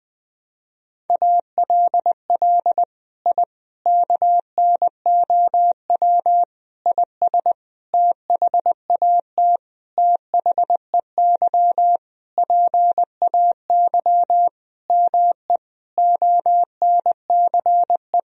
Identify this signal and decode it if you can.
Morse code, heard as a keyed tone, about 21 wpm, 700 Hz. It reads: ALL I KNOW IS THAT THEY PAY ME ONCE
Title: Sentences from Top 1000 Words - Review Part 1 - 20wpm